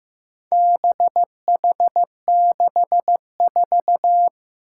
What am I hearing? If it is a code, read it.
Morse code, heard as a keyed tone, about 15 wpm, 700 Hz. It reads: BH64
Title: Four Challenging Letter-Number Combinations 15wpm